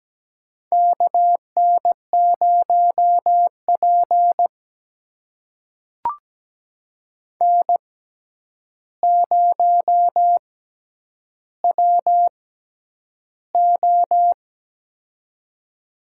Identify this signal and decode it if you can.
Morse code, heard as a keyed tone, about 17 wpm, spaced as for 9 wpm, 700 Hz. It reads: KN0P N0WO
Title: USA Call Signs - 4 Characters - 6x Character Spacing - 17wpm